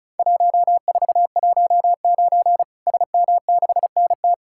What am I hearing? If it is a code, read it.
Morse code, heard as a keyed tone, about 36 wpm, 700 Hz. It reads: 1419 SM6NT
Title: Worked All Europe DX Contest - QTC Exchange - 35wpm